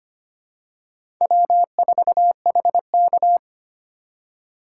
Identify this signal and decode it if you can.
Morse code, heard as a keyed tone, about 25 wpm, 700 Hz. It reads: W4HK
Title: USA Call Signs - 4 Characters - 3x Character Spacing - 25wpm